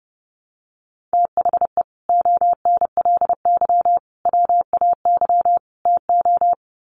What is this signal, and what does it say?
Morse code, heard as a keyed tone, about 30 wpm, 700 Hz. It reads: THE ONLY WAY TO
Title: Sets of 4 Words 30wpm